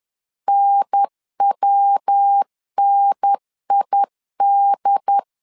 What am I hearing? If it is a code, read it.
Morse code, heard as a keyed tone, about 10 wpm, 800 Hz. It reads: NWNID
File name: LEZIONI / 3 MWRO / 3 MW 10wpm 1959